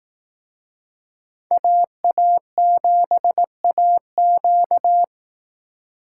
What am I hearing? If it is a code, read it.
Morse code, heard as a keyed tone, about 18 wpm, 700 Hz. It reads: AA7AQ
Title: Speed Racing - USA 2x2 Call Signs 15wpm